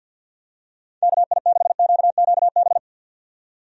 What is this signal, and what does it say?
Morse code, heard as a keyed tone, about 50 wpm, 700 Hz. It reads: KI6QQB
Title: USA Call Signs - 6 Characters - 50wpm